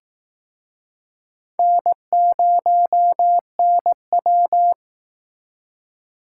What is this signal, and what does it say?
Morse code, heard as a keyed tone, about 18 wpm, 700 Hz. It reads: N0NW
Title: Speed Racing - USA 1x2 Call Signs 15wpm